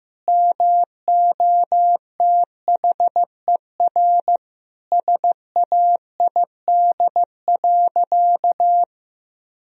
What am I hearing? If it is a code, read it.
Morse code, heard as a keyed tone, about 15 wpm, 700 Hz. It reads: MOTHER SAID.
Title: Sentences from Top 1000 Words - Review Part 3 - 15wpm